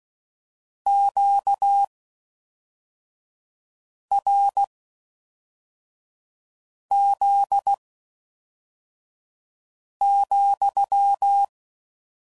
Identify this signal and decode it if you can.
Morse code, heard as a keyed tone, about 16 wpm, spaced as for 4 wpm, 800 Hz. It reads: QRZ,